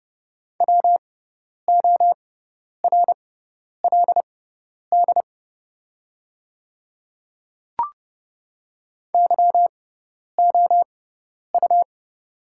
Morse code, heard as a keyed tone, about 30 wpm, spaced as for 10 wpm, 700 Hz. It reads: WORLD YOU
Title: 6x Character Spacing - Top 200 Words - 30wpm